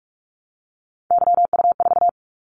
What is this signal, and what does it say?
Morse code, heard as a keyed tone, about 46 wpm, 700 Hz. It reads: YU4
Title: Callsign Prefixes for Top 25 DXCC - 45wpm